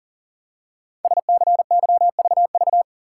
Morse code, heard as a keyed tone, about 42 wpm, 690 Hz. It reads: SCYVV
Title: Code Groups - 5 Letters - 40wpm